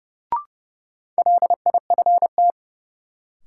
Morse code, heard as a keyed tone, about 30 wpm, 700 Hz. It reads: LIFT
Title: Sentences from Top 1000 Words - New Word Drill - 30wpm